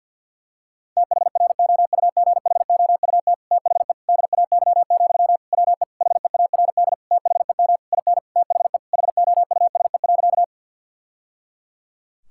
Morse code, heard as a keyed tone, about 50 wpm, 700 Hz. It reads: THROUGHOUT THE DAY, WE HEARD THEM IN THE HOUSE.
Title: Sentences from Top 1000 Words - Review Part 5 - 50wpm